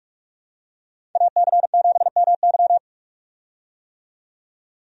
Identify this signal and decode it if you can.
Morse code, heard as a keyed tone, about 46 wpm, 700 Hz. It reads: AC7MY
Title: USA Call Signs - 5 Characters - 45wpm